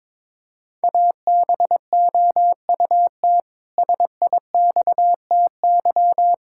Morse code, heard as a keyed tone, about 22 wpm, 700 Hz. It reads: ABOUT SIXTY
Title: Pangram Challenge v2 - 22wpm